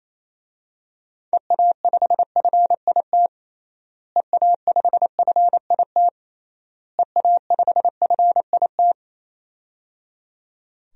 Morse code, heard as a keyed tone, about 28 wpm, 700 Hz. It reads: EA5FIT EA5FIT EA5FIT
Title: DXCC-100 Call Signs - 6 Characters - Repeated 3x - 28wpm